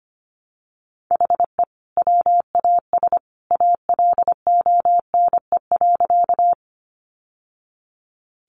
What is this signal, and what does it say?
Morse code, heard as a keyed tone, about 25 wpm, 700 Hz. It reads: HE WAS ALONE.
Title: Sentences from Top 1000 Words 25wpm